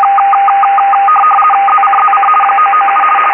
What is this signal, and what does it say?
RTTY, about 50 baud. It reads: ZCZC 997
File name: uploader / 11038 usb short